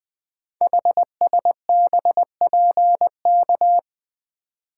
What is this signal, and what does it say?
Morse code, heard as a keyed tone, about 20 wpm, 700 Hz. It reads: HSBPK
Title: Code Groups - 5 Letters - 20wpm